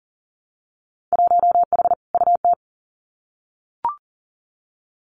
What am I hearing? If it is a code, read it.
Morse code, heard as a keyed tone, about 40 wpm, 700 Hz. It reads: 1H UT
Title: Winter Field Day Exchanges 40wpm